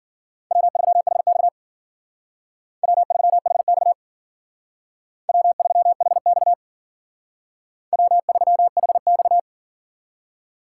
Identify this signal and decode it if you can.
Morse code, heard as a keyed tone, about 51 wpm, 700 Hz. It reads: W3HX W3HX W3HX W3HX